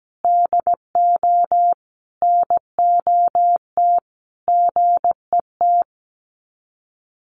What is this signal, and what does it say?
Morse code, heard as a keyed tone, about 17 wpm, 700 Hz. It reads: DO NOT GET
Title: Sets of 5 Words Encore2 17wpm